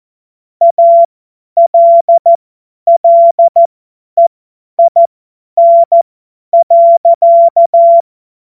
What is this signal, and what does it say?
Morse code, heard as a keyed tone, about 14 wpm, spaced as for 10 wpm, 680 Hz. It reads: ALLEIN.
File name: Morsekurs / Witze / Witz 22. Februar 2022 50 F70